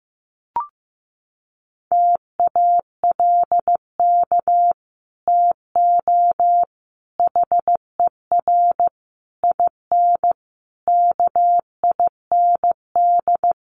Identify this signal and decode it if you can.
Morse code, heard as a keyed tone, about 15 wpm, 700 Hz. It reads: TALK TO HER IN KIND
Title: Sentences from Top 1500 Words 15wpm